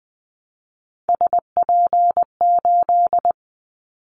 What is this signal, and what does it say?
Morse code, heard as a keyed tone, about 20 wpm, 700 Hz. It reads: SP8
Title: Callsign Prefixes for Top 25 DXCC - Repeated 3x - 20wpm